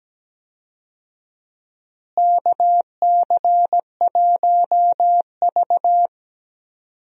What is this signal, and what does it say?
Morse code, heard as a keyed tone, about 17 wpm, 700 Hz. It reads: KC1V
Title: USA Call Signs - 4 Characters - Repeated 2x - 17wpm